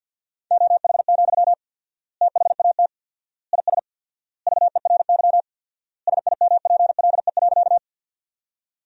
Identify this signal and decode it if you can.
Morse code, heard as a keyed tone, about 52 wpm, 700 Hz. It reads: OH, THAT IS VERY SIMPLE.
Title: 4x Word-Spacing - Sentences from Top 500 Words - 50wpm